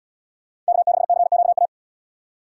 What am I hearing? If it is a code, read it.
Morse code, heard as a keyed tone, about 77 wpm, 700 Hz. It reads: K6CQA